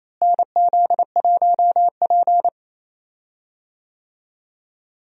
Morse code, heard as a keyed tone, about 28 wpm, 700 Hz. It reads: NZ1P
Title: USA Call Signs - 4 Characters - Repeated 2x - 28wpm